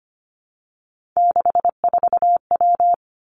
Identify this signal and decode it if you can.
Morse code, heard as a keyed tone, about 25 wpm, 700 Hz. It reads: 64W
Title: Three Challenging Letter-Number Combinations 25wpm